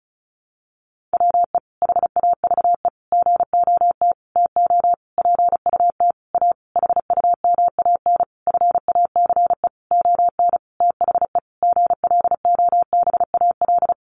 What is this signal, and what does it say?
Morse code, heard as a keyed tone, about 34 wpm, 700 Hz. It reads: WE HAVE GOT TO PUT A HUMAN FACE ON THE GLOBAL
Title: Sentences from Top 1200 Words 35wpm